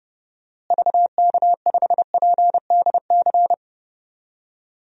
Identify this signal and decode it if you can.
Morse code, heard as a keyed tone, about 30 wpm, 700 Hz. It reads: VK5PDC
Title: DXCC-100 Call Signs - 6 Characters - Repeated 2x - 30wpm